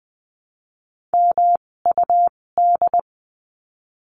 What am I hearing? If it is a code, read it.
Morse code, heard as a keyed tone, about 20 wpm, spaced as for 16 wpm, 700 Hz. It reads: MUD
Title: 256 Lesson 42a - Words 12-20wpm